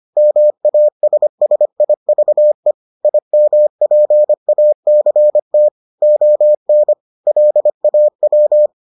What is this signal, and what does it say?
Morse code, heard as a keyed tone, about 25 wpm, 600 Hz. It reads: MASSIVE IMPACT ON LAW